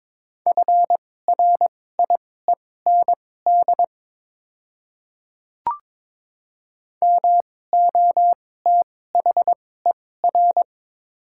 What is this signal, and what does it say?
Morse code, heard as a keyed tone, about 22 wpm, spaced as for 16 wpm, 700 Hz. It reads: FRIEND MOTHER
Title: 2x Character Spacing - Top 500 Words - 22wpm